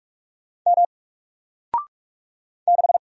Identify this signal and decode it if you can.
Morse code, heard as a keyed tone, about 44 wpm, 700 Hz. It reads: M 6
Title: Single Character (Letter:Number:Punctuation:Prosign) 45wpm